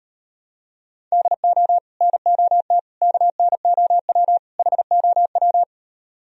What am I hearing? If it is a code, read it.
Morse code, heard as a keyed tone, about 37 wpm, 700 Hz. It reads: DO NOT KNOW HOW